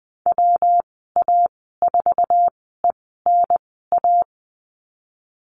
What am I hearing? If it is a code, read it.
Morse code, heard as a keyed tone, about 20 wpm, spaced as for 15 wpm, 700 Hz. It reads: WA4ENA
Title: 173 Lesson 28g - Number 9 - Call Signs 20wpm